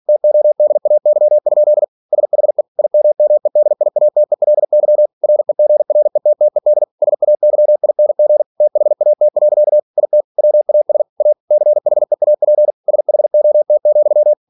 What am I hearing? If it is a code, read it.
Morse code, heard as a keyed tone, about 46 wpm, 600 Hz. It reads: TODAY? SHE IMMEDIATELY REGRETTED SAYING THAT. IT WAS A CHEAP SHOT,
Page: Never 47wpm